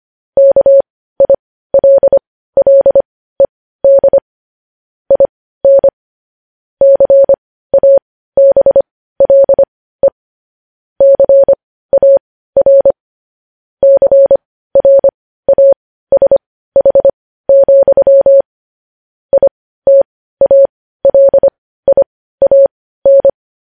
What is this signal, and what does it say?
Morse code, heard as a keyed tone, about 25 wpm, spaced as for 15 wpm, 550 Hz. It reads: KILLED IN CABLE CAR CRASH, ITALIAN
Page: news headlines in morse code at 15 wpm